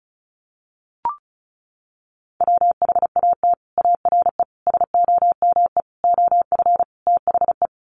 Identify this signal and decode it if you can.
Morse code, heard as a keyed tone, about 34 wpm, 700 Hz. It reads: WHAT ARE SOME OF THE
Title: Sets of 5 Words Encore 35wpm